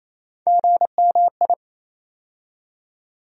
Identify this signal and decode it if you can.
Morse code, heard as a keyed tone, about 28 wpm, 700 Hz. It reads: GMI
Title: USA Call Sign Postfixes - 3 Letters 28wpm